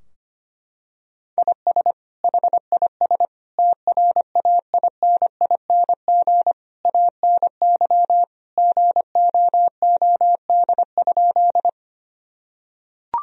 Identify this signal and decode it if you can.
Morse code, heard as a keyed tone, about 25 wpm, 700 Hz. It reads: IS HIS TRAINING ANY GOOD?
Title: Sentences from Top 1000 Words - Review Part 4 - 25wpm